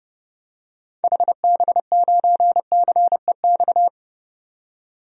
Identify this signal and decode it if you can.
Morse code, heard as a keyed tone, about 30 wpm, 700 Hz. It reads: HB9CEX
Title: DXCC-100 Call Signs - 6 Characters - 30wpm